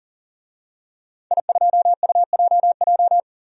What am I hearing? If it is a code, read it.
Morse code, heard as a keyed tone, about 40 wpm, 700 Hz. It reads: I2UJJ
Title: DXCC-100 Call Signs - 5 Characters - 40wpm